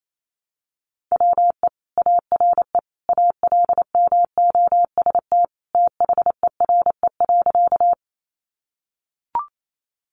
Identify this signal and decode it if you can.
Morse code, heard as a keyed tone, about 28 wpm, 700 Hz. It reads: WE ARE ALMOST THERE.
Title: Sentences from Top 700 Words 28wpm